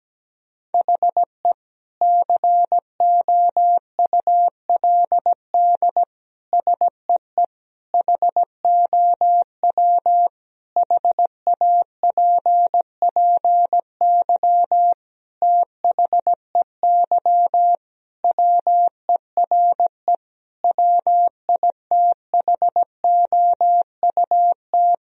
Morse code, heard as a keyed tone, about 17 wpm, 700 Hz. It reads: HE COULD SEE HOW HAPPY THEY WERE WITHOUT
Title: Sentences from Top 1000 Words - Review Part 3 - 17wpm